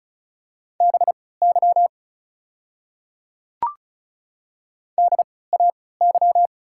Morse code, heard as a keyed tone, about 36 wpm, spaced as for 20 wpm, 700 Hz. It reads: BY DAY
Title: 3x Character Spacing - Top 100 Words - 35wpm